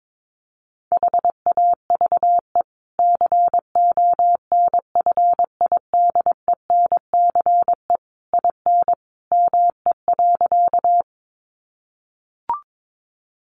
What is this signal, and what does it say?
Morse code, heard as a keyed tone, about 22 wpm, 700 Hz. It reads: HAVE CONFIDENCE IN ME.
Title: Sentences from Top 1500 Words 22wpm